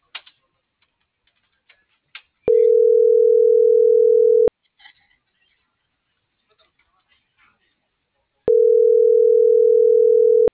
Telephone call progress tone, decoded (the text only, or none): ringback tone